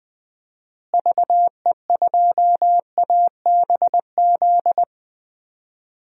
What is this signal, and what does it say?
Morse code, heard as a keyed tone, about 21 wpm, 700 Hz. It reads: VE2ABZ
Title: DXCC-100 Call Signs - 6 Characters - Repeated 2x - 20wpm